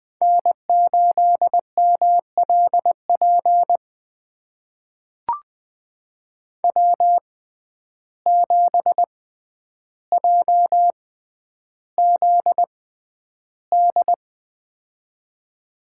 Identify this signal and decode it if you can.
Morse code, heard as a keyed tone, about 20 wpm, spaced as for 10 wpm, 700 Hz. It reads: N8MLP W7JZD
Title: USA Call Signs - 5 Characters - 6x Character Spacing - 20wpm